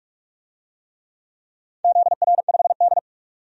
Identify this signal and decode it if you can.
Morse code, heard as a keyed tone, about 46 wpm, 700 Hz. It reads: ZR5D
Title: DXCC-100 Call Signs - 4 Characters - 45wpm